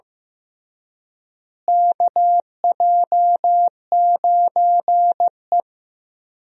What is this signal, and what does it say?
Morse code, heard as a keyed tone, about 16 wpm, 700 Hz. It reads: KJ9E
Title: USA Call Signs - 4 Characters - Repeated 3x - 15wpm